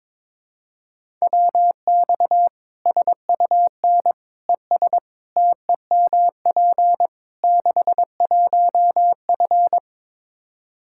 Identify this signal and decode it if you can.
Morse code, heard as a keyed tone, about 22 wpm, 700 Hz. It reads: WX SUN ES TEMP 61F
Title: Practice QSO - Exchange 2 - Info - 22wpm